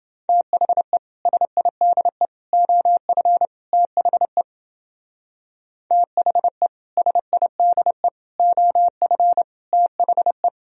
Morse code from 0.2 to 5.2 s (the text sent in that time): THE SIDE OF THE